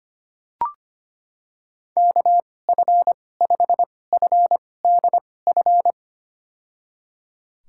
Morse code, heard as a keyed tone, about 25 wpm, spaced as for 18 wpm, 700 Hz. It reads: KF5FDF
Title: USA Call Signs - 6 Characters - 2x Character Spacing - 25wpm